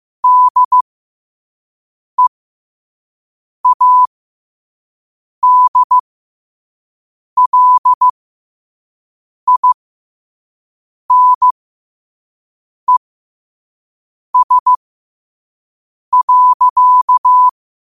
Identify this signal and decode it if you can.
Morse code, heard as a keyed tone, about 15 wpm, spaced as for 5 wpm, 1000 Hz. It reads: DEADLINES.
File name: qotdfiles / 5wpm